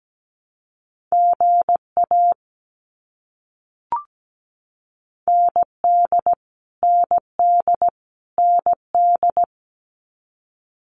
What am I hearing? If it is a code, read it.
Morse code, heard as a keyed tone, about 17 wpm, 700 Hz. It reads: GA ND ND ND
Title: US States Repeated 3x 17wpm